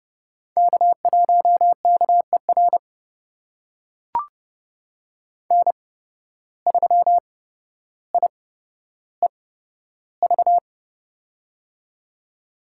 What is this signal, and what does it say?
Morse code, heard as a keyed tone, about 30 wpm, spaced as for 13 wpm, 700 Hz. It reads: K1KER N3IEV
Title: USA Call Signs - 5 Characters - 8x Character Spacing - 30wpm